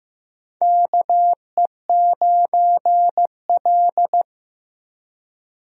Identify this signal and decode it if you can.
Morse code, heard as a keyed tone, about 15 wpm, 700 Hz. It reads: KE9L
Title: USA Call Signs - 4 Characters - Repeated 3x - 15wpm